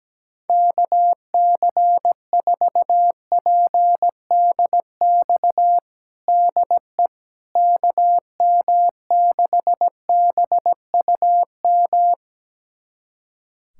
Morse code, heard as a keyed tone, about 17 wpm, 700 Hz. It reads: KC4PDX DE KM6BUM
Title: USA Callsign DE Callsign 17wpm